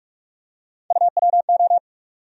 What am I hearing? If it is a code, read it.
Morse code, heard as a keyed tone, about 47 wpm, 700 Hz. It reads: UWO
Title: USA Call Sign Postfixes - 3 Letters 45wpm